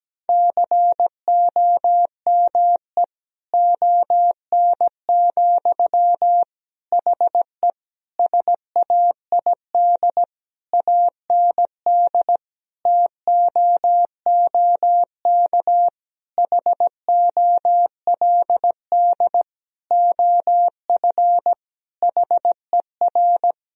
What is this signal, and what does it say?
Morse code, heard as a keyed tone, about 17 wpm, 700 Hz. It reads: COME ON, HE SAID AND TOOK HOLD OF HER
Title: Sentences from Top 1000 Words - Review Part 2 - 17wpm